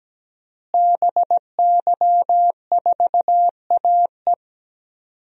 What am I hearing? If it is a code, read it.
Morse code, heard as a keyed tone, about 17 wpm, 700 Hz. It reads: BY4AE
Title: DXCC-100 Call Signs - 5 Characters - Repeated 2x - 17wpm